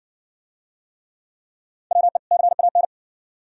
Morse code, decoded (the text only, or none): WE CAN